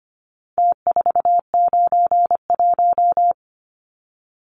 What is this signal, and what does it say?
Morse code, heard as a keyed tone, about 25 wpm, 700 Hz. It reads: T491
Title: Numbers - Contest Style - 4-Digit 25wpm